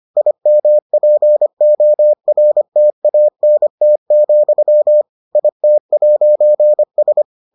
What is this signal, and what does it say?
Morse code, heard as a keyed tone, about 25 wpm, 600 Hz. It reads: IMPORTANT, IT'S